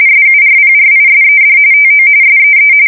RTTY, about 45.45 baud. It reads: SURPRISED TO SEE
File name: fsk170 / USS-New-Jersey